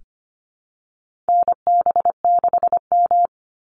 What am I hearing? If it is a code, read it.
Morse code, heard as a keyed tone, about 25 wpm, 700 Hz. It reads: NB6M
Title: Call Signs Repeated 3x - 25wpm